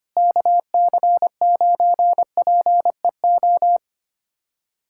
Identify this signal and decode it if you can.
Morse code, heard as a keyed tone, about 25 wpm, 700 Hz. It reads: KC9PEO
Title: USA Callsign DE Callsign 25wpm